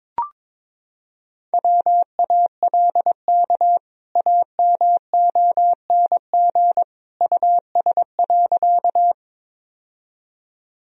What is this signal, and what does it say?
Morse code, heard as a keyed tone, about 21 wpm, 700 Hz. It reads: WALK AMONG US.
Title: Sentences from Top 1000 Words - Review Part 3 - 22wpm